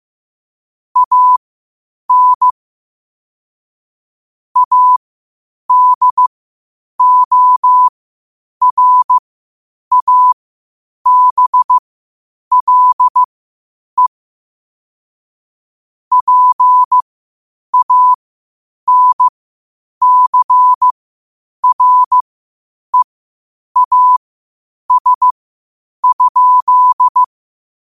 Morse code, heard as a keyed tone, about 15 wpm, spaced as for 8 wpm, 1000 Hz. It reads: AN ADORABLE PANCREAS?
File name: qotdfiles / 8wpm